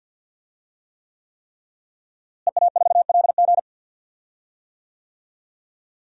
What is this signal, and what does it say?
Morse code, heard as a keyed tone, about 52 wpm, 700 Hz. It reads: EA4LG
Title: DXCC-100 Call Signs - 5 Characters - Repeated 3x - 50wpm